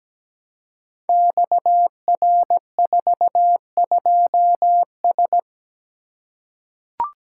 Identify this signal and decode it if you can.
Morse code, heard as a keyed tone, about 17 wpm, 700 Hz. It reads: XR42S
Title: Code Groups - 5 Letter:Numbers - 17wpm